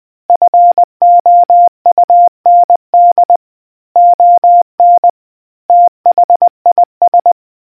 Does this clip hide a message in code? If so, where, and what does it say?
Yes: Morse code from 0.3 to 7.3 s, heard as a keyed tone, about 20 wpm, 700 Hz. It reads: FOUND ON THIS